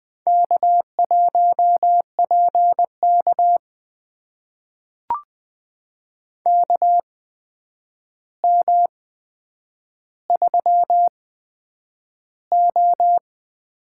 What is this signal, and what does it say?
Morse code, heard as a keyed tone, about 20 wpm, spaced as for 9 wpm, 700 Hz. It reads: K1PK KM3O
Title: USA Call Signs - 4 Characters - 8x Character Spacing - 20wpm